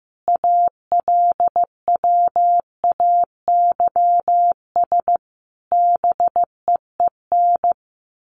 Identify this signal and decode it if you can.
Morse code, heard as a keyed tone, about 15 wpm, 700 Hz. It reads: ALWAYS BEEN
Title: Sentences from Top 1400 Words 15wpm